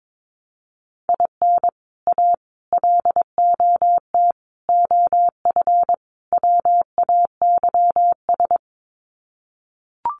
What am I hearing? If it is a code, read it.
Morse code, heard as a keyed tone, about 22 wpm, 700 Hz. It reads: IN A LOT OF WAYS
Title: Sets of 5 Words Encore 22wpm